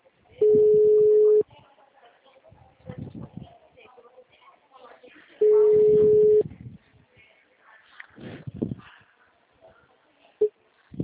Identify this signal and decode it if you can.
Ringback tone (Europe, 425 Hz), with speech.